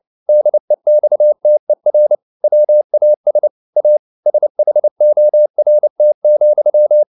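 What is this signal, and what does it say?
Morse code, heard as a keyed tone, about 29 wpm, 600 Hz. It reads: DEXTER WAS A SHORT,
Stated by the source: Morse code at 29 wpm, sending Ken Follett's novel Never in English, from CW med Gnister.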